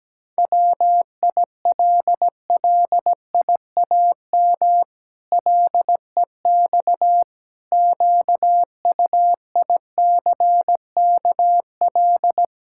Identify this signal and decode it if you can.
Morse code, heard as a keyed tone, about 17 wpm, 700 Hz. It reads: WILLIAM LEX QUICKL
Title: Pangram Challenge v2 - 17wpm